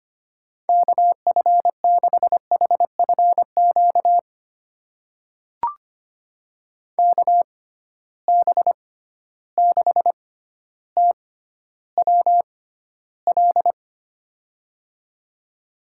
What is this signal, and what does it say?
Morse code, heard as a keyed tone, about 25 wpm, spaced as for 13 wpm, 700 Hz. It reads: KF6HFQ KB6TWL
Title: USA Call Signs - 6 Characters - 6x Character Spacing - 25wpm